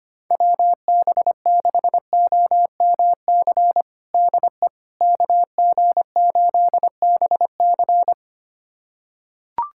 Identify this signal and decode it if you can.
Morse code, heard as a keyed tone, about 25 wpm, 700 Hz. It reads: WB6OMC DE KG8BC
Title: USA Callsign DE Callsign 25wpm